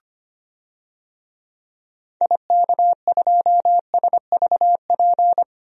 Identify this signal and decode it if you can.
Morse code, heard as a keyed tone, about 25 wpm, 700 Hz. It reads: IK2SVP
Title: DXCC-100 Call Signs - 6 Characters - 25wpm